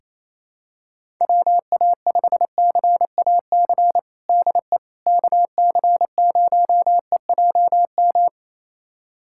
Morse code, heard as a keyed tone, about 28 wpm, 700 Hz. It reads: WA5CAC DE KC0EJM
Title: USA Callsign DE Callsign 28wpm